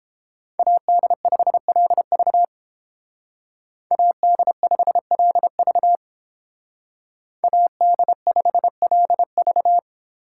Morse code, heard as a keyed tone, about 31 wpm, 700 Hz. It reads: AD5LV AD5LV AD5LV